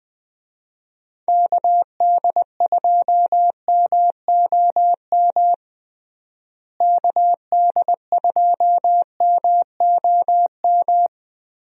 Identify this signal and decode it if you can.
Morse code, heard as a keyed tone, about 20 wpm, 700 Hz. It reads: KD2MOM KD2MOM
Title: USA Call Signs - 6 Characters - Repeated 3x - 20wpm